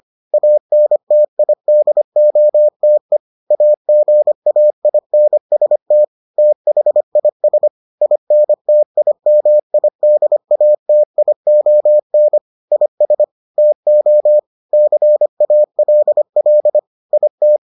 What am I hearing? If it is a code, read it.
Morse code, heard as a keyed tone, about 25 wpm, 600 Hz. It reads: ANTIDOTE AGAINST THIS INTIMIDATION IS TO CALL IT